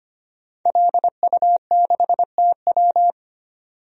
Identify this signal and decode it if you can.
Morse code, heard as a keyed tone, about 25 wpm, 700 Hz. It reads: LU6TW
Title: DXCC-100 Call Signs - 5 Characters - 25wpm